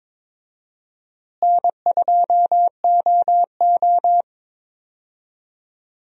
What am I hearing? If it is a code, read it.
Morse code, heard as a keyed tone, about 22 wpm, 700 Hz. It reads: N2OO
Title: USA Call Signs - 4 Characters - Repeated 3x - 22wpm